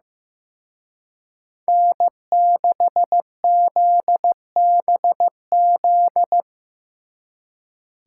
Morse code, heard as a keyed tone, about 17 wpm, 700 Hz. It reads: N6ZBZ
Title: Speed Racing - USA 1x3 Call Signs 15wpm